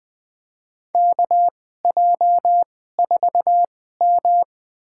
Morse code, heard as a keyed tone, about 20 wpm, spaced as for 14 wpm, 700 Hz. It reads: KJ4M
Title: USA Call Signs - 4 Characters - 2x Character Spacing - 20wpm